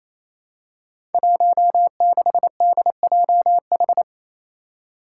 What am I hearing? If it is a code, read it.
Morse code, heard as a keyed tone, about 28 wpm, 700 Hz. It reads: 16DJH
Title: Five Challenging Letter-Number Combinations 28wpm